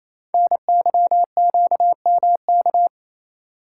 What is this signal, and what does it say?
Morse code, heard as a keyed tone, about 28 wpm, 700 Hz. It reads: NYQMK
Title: Code Groups - 5 Letters - 28wpm